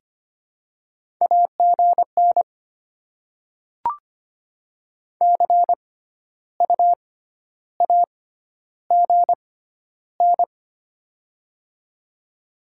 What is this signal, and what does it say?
Morse code, heard as a keyed tone, about 25 wpm, spaced as for 12 wpm, 700 Hz. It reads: AGN CU AGN
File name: Top 20 QSO Elements - 6x Character Spacing - 25wpm